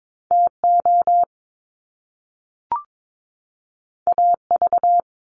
Morse code, heard as a keyed tone, about 22 wpm, 700 Hz. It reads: TO AV
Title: Top Two Letter Combinations in Words 22wpm